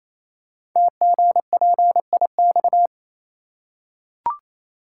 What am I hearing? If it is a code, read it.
Morse code, heard as a keyed tone, about 27 wpm, 700 Hz. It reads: TGPIX
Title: Code Groups - 5 Letters - 28wpm